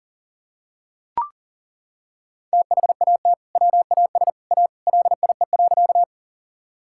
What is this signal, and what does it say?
Morse code, heard as a keyed tone, about 40 wpm, 700 Hz. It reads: THAT WAS A LIE.
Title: Sentences from Top 1600 Words 40wpm